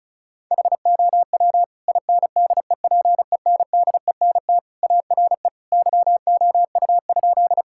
Morse code, heard as a keyed tone, about 36 wpm, 700 Hz. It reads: HOW INDEPENDENT ARE YOU?
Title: Sentences from Top 1000 Words - Review Part 6 - 35wpm